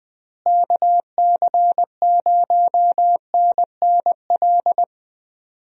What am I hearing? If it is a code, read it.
Morse code, heard as a keyed tone, about 20 wpm, 700 Hz. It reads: KC0NNL
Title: USA Call Signs - 6 Characters - Repeated 3x - 20wpm